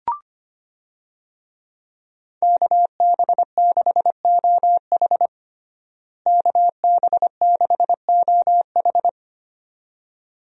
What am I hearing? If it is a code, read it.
Morse code, heard as a keyed tone, about 25 wpm, 700 Hz. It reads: KB6OH KB6OH
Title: USA Call Signs - 5 Characters - Repeated 2x - 25wpm